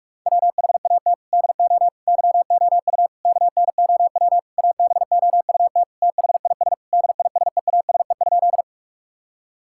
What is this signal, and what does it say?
Morse code, heard as a keyed tone, about 46 wpm, 700 Hz. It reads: WHAT DO YOU KNOW ABOUT THIS DISEASE?
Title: Sentences from Top 1000 Words - Review Part 6 - 45wpm